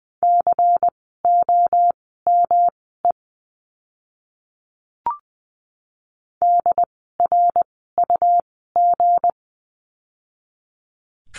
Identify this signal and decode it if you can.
Morse code, heard as a keyed tone, about 20 wpm, spaced as for 14 wpm, 700 Hz. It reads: COME DRUG
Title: 160 Lesson 26f - Letter g - Words 20wpm